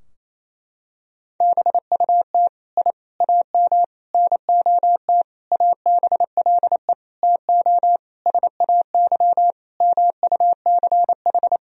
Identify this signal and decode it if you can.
Morse code, heard as a keyed tone, about 28 wpm, 700 Hz. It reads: BUT I AM NOT ABLE TO SAY MUCH
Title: Sentences from Top 1000 Words - Review Part 2 - 28wpm